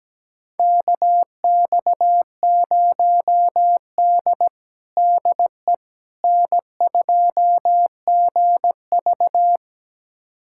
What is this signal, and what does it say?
Morse code, heard as a keyed tone, about 17 wpm, 700 Hz. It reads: KX0D DE N2GV
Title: USA Callsign DE Callsign - 4 Characters - 17wpm